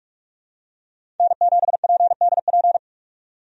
Morse code, heard as a keyed tone, about 46 wpm, 700 Hz. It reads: N7PDP
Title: USA Call Signs - 5 Characters - 45wpm